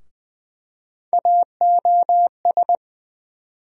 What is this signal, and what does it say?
Morse code, heard as a keyed tone, about 21 wpm, 700 Hz. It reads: AOS
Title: USA Call Sign Postfixes - 3 Letters 20wpm